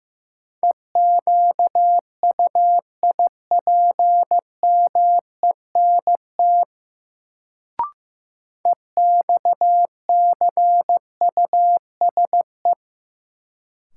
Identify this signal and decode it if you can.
Morse code, heard as a keyed tone, about 15 wpm, 700 Hz. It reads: EQUIPMENT EXCUSE
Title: Sentences from Top 700 Words - New Word Drill - 15wpm